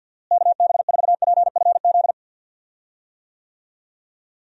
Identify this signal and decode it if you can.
Morse code, heard as a keyed tone, about 52 wpm, 700 Hz. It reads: KB4PFZ
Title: USA Call Signs - 6 Characters - Repeated 2x - 50wpm